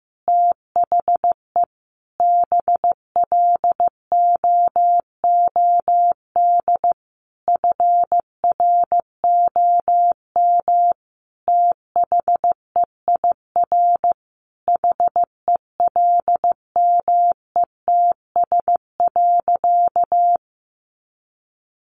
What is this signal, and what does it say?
Morse code, heard as a keyed tone, about 15 wpm, 700 Hz. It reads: THE BLOOD FROM THEIR HELMETS.
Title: A Sound Of Thunder 15wpm